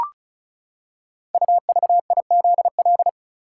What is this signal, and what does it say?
Morse code, heard as a keyed tone, about 36 wpm, 700 Hz. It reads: UVIZL
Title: Code Groups - 5 Letters - 35wpm